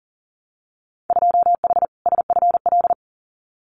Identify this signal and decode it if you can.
Morse code, heard as a keyed tone, about 40 wpm, 700 Hz. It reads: 2H SFL
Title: Winter Field Day Exchanges Encore 40wpm